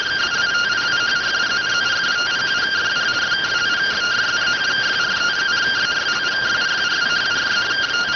Two-tone teleprinter signal, about 170 Hz shift.